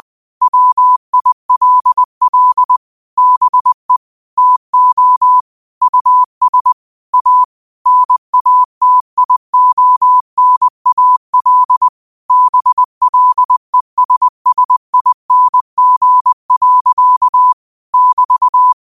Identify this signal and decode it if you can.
Morse code, heard as a keyed tone, about 20 wpm, 1000 Hz. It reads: WILL BE TO US A NATIONAL BLESSING. <BT>
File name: qotdfiles / 20wpm